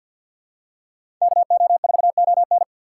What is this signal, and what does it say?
Morse code, heard as a keyed tone, about 53 wpm, 690 Hz. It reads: KO4ON
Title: USA Call Signs - 6x Character Spacing - 50wpm